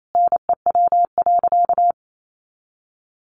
Morse code, heard as a keyed tone, about 28 wpm, 700 Hz. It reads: NEW.
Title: Sentences from Top 1300 Words 28wpm